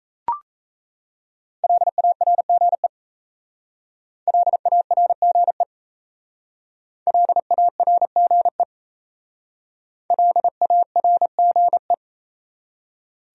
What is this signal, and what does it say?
Morse code, heard as a keyed tone, about 34 wpm, 700 Hz. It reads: LARGE LARGE LARGE LARGE
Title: Speed-Racing Top 200 Words 28wpm